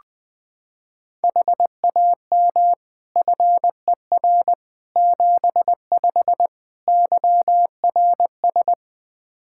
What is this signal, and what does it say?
Morse code, heard as a keyed tone, about 20 wpm, 700 Hz. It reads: HAM FER 75 YRS
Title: K1USN SST QSO Practice 20wpm